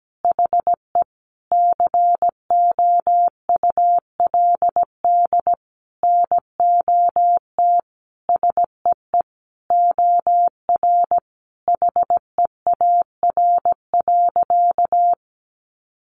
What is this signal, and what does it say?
Morse code, heard as a keyed tone, about 17 wpm, 700 Hz. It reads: HE COULD NOT SEE OR HEAR.
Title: Sentences from Top 600 Words 17wpm